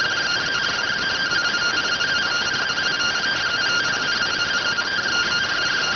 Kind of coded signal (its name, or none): two-tone teleprinter signal